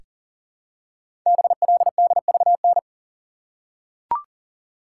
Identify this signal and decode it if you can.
Morse code, heard as a keyed tone, about 41 wpm, 700 Hz. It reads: BLDVN
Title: Code Groups - 5 Letter:Numbers - 40wpm